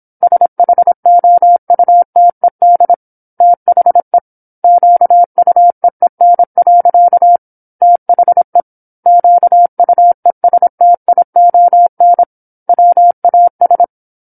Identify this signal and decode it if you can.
Morse code, heard as a keyed tone, about 26 wpm, 700 Hz. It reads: SHOUTED THE QUEEN. THE QUESTION WAS
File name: 08-Chapter-08 26 WPM